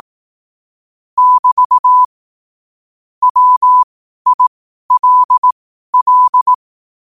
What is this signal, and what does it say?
Morse code, heard as a keyed tone, about 18 wpm, spaced as for 12 wpm, 1000 Hz. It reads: <BT> WILL
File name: qotdfiles / 12wpm